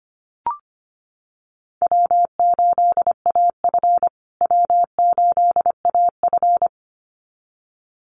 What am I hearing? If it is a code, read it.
Morse code, heard as a keyed tone, about 25 wpm, 700 Hz. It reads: W8AF W8AF
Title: USA Call Signs Repeated 2x 25wpm